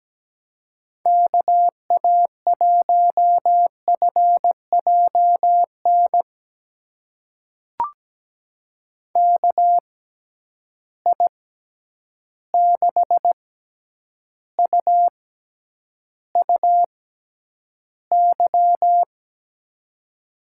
Morse code, heard as a keyed tone, about 17 wpm, spaced as for 9 wpm, 700 Hz. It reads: KA1FJN KI6UUY
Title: USA Call Signs - 6 Characters - 6x Character Spacing - 17wpm